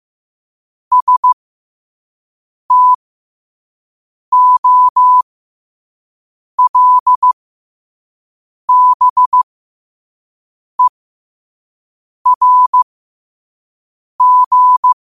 Morse code, heard as a keyed tone, about 15 wpm, spaced as for 5 wpm, 1000 Hz. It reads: STOLBERG
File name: qotdfiles / 5wpm